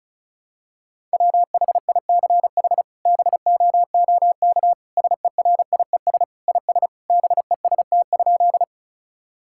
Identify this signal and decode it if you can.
Morse code, heard as a keyed tone, about 36 wpm, 700 Hz. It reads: WHICH BOOK SERIES IS BEST?
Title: Sentences from Top 1000 Words - Review Part 5 - 35wpm